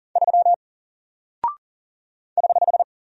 Morse code, heard as a keyed tone, about 40 wpm, 700 Hz. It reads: 3 <HH>
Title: Single Character (Letter:Number:Punctuation:Prosign) 40wpm